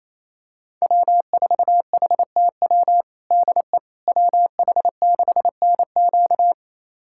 Morse code, heard as a keyed tone, about 28 wpm, 700 Hz. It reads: W4HTW DE WH6NQ
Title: USA Callsign DE Callsign - 5 Characters - 28wpm